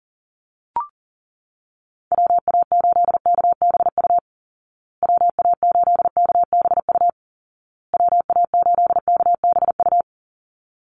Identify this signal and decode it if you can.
Morse code, heard as a keyed tone, about 40 wpm, 700 Hz. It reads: WA8KBU WA8KBU WA8KBU
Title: Call Signs Repeated 2x - 40wpm